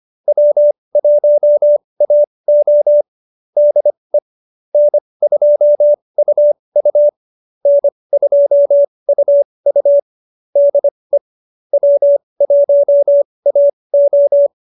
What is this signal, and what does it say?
Morse code, heard as a keyed tone, about 25 wpm, spaced as for 20 wpm, 590 Hz. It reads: W1AO DE N2UU N2UU DE W1AO
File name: QSO313 20